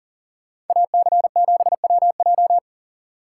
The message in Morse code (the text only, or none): AC7WJ